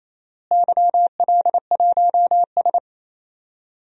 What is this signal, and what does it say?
Morse code, heard as a keyed tone, about 28 wpm, 700 Hz. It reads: YL1S
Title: DXCC-100 Call Signs - 4 Characters - Repeated 2x - 28wpm